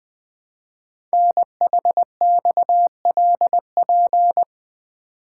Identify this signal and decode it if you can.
Morse code, heard as a keyed tone, about 21 wpm, 700 Hz. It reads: NHXLP
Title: Code Groups - 5 Letter:Numbers - 20wpm